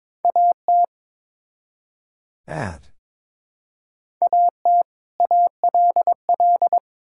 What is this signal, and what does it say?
Morse code, heard as a keyed tone, about 22 wpm, 700 Hz. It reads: AT AT ALL
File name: Build A Phrase - 2-Word Phrases - 22wpm